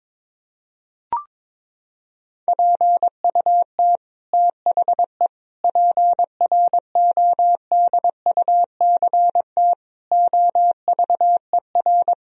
Morse code, heard as a keyed tone, about 22 wpm, 700 Hz. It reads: PUT THE PRODUCT OVER
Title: Sentences from Top 1000 Words - Review Part 2 - 22wpm